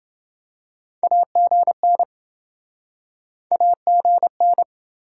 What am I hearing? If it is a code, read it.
Morse code, heard as a keyed tone, about 30 wpm, 700 Hz. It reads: AGN AGN